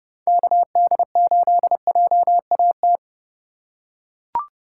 Morse code, heard as a keyed tone, about 31 wpm, 700 Hz. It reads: KD8JAT
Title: USA Call Signs - 6x Character Spacing - 30wpm